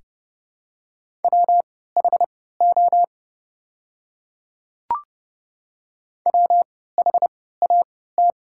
Morse code, heard as a keyed tone, about 30 wpm, spaced as for 17 wpm, 700 Hz. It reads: WHO WHAT
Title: 3x Character Spacing - Top 100 Words - 30wpm